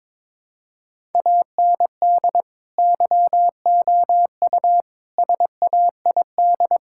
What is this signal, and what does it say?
Morse code, heard as a keyed tone, about 22 wpm, 700 Hz. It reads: AND YOU SAID
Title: Sentences from Top 1000 Words - Review Part 5 - 22wpm